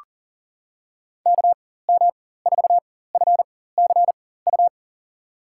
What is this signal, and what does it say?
Morse code, heard as a keyed tone, about 40 wpm, spaced as for 19 wpm, 700 Hz. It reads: KM4FCU
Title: USA Call Signs - 4x Character Spacing - 40wpm